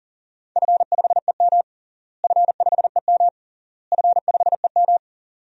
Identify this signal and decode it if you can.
Morse code, heard as a keyed tone, about 40 wpm, 700 Hz. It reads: F5EM F5EM F5EM
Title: DXCC-100 Call Signs - 4 Characters - Repeated 3x - 40wpm